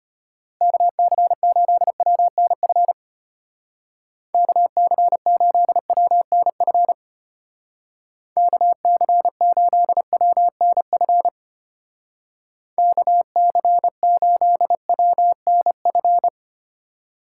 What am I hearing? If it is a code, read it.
Morse code, heard as a keyed tone, about 31 wpm, 700 Hz. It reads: KC8WNF KC8WNF KC8WNF KC8WNF
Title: Speed Racing - USA 2x3 Call Signs 25wpm